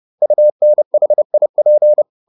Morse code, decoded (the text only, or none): UNHIP